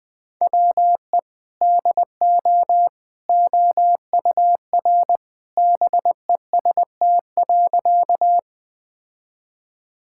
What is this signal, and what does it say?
Morse code, heard as a keyed tone, about 21 wpm, 700 Hz. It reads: WE DO OUR BEST.
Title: Sentences with 2-Word Phrases - 20wpm